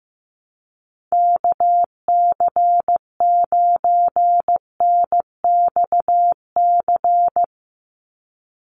Morse code, heard as a keyed tone, about 15 wpm, 700 Hz. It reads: KC9NXC
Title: USA Call Signs 15wpm